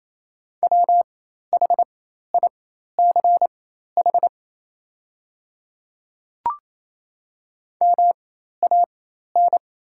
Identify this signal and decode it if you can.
Morse code, heard as a keyed tone, about 28 wpm, spaced as for 13 wpm, 690 Hz. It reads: WHICH MAN
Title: 4x Character Spacing - Top 500 Words - 28wpm